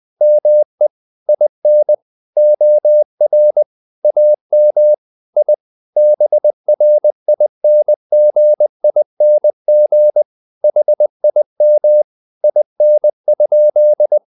Morse code, heard as a keyed tone, about 20 wpm, 600 Hz. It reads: ME IN OR AM I BRINGING HIM IN?
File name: the old man and the sea part 2 20wpm